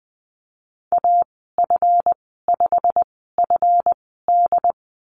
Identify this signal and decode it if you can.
Morse code, heard as a keyed tone, about 20 wpm, spaced as for 14 wpm, 700 Hz. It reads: AF5FD
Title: 238 Lesson 38g - Letter q - Call Signs 20wpm